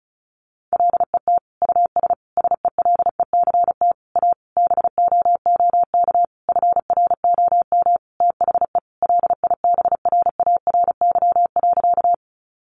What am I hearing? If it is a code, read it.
Morse code, heard as a keyed tone, about 34 wpm, 700 Hz. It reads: LET US SELECT A BOOK FROM THE LIBRARY.
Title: Sentences from Top 1300 Words 35wpm